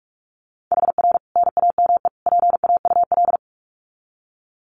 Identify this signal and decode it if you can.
Morse code, heard as a keyed tone, about 44 wpm, 700 Hz. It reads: HR NAME PAUL
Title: Common QSO Phrases 45wpm